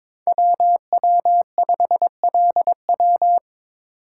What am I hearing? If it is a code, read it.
Morse code, heard as a keyed tone, about 22 wpm, 700 Hz. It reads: WW5LW
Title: USA Call Signs - 3x Character Spacing - 22wpm